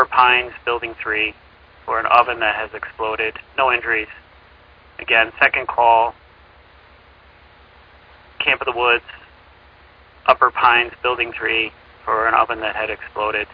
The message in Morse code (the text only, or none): none